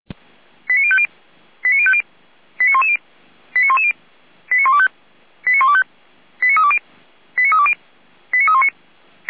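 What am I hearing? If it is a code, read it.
Five-tone selective call (ZVEI), heard as quick runs of five tones, tones of about 70 ms. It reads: 89050 89050 89110 89110 89125 89125 89320 89320 89319
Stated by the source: Five-tone ZVEI selective call received on an SDR.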